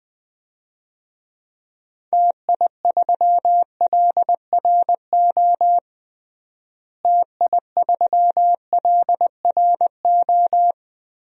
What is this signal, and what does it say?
Morse code, heard as a keyed tone, about 20 wpm, 690 Hz. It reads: TI3LRO TI3LRO
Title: DXCC-100 Call Signs - 6 Characters - Repeated 2x - 20wpm